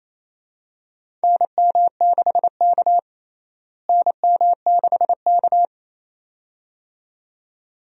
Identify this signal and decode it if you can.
Morse code, heard as a keyed tone, about 28 wpm, 700 Hz. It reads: NM6K NM6K
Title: USA Call Signs - 4 Characters - Repeated 2x - 28wpm